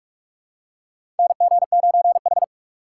Morse code, heard as a keyed tone, about 46 wpm, 700 Hz. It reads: NG9H
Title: USA Call Signs - 4 Characters - 6x Character Spacing - 45wpm